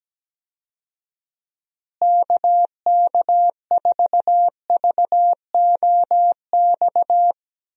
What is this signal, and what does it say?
Morse code, heard as a keyed tone, about 17 wpm, 700 Hz. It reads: KK4VOX
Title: USA Call Signs - 6 Characters - Repeated 2x - 17wpm